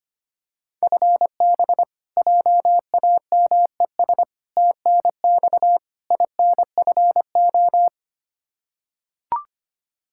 Morse code, heard as a keyed tone, about 25 wpm, 700 Hz. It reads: FB JAMES TNX INFO
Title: Practice QSO - Exchange 3 - More Info - 25wpm